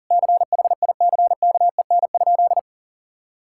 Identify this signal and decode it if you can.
Morse code, heard as a keyed tone, about 40 wpm, 700 Hz. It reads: CHICKEN?
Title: Sentences from Top 1000 Words - Review Part 5 - 40wpm